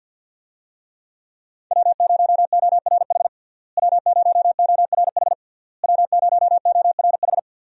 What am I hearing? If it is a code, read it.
Morse code, heard as a keyed tone, about 50 wpm, 700 Hz. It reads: W0ORH W0ORH W0ORH
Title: USA Call Signs - 5 Characters - Repeated 3x - 50wpm